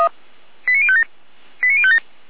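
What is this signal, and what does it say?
Five-tone selective call (ZVEI), heard as quick runs of five tones, tones of about 70 ms. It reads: 89057 89057
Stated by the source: ZVEI five-tone selective call 89115 received by SDR.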